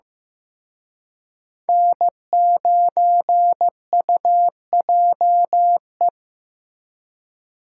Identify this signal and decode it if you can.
Morse code, heard as a keyed tone, about 18 wpm, 700 Hz. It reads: N9UJE
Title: Speed Racing - USA 1x3 Call Signs 15wpm